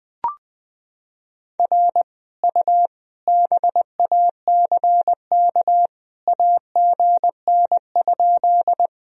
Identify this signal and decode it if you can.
Morse code, heard as a keyed tone, about 20 wpm, 700 Hz. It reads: R U BACK AGN?
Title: Sentences from Top 200 Words and Abbreviations 20wpm